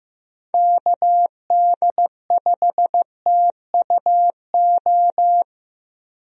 Morse code, heard as a keyed tone, about 15 wpm, 700 Hz. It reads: KD5TUO
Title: USA Call Signs - 6 Characters - Repeated 2x - 15wpm